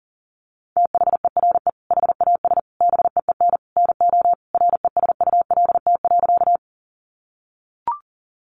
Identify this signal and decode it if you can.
Morse code, heard as a keyed tone, about 40 wpm, 700 Hz. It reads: THERE HAS BEEN NO RESULT.
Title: Sentences from Top 700 Words 40wpm